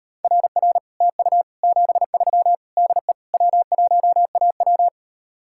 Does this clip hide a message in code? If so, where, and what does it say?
Yes: Morse code from 0.2 to 4.9 s, heard as a keyed tone, about 39 wpm, 700 Hz. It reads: RR TU 73 DE W1AW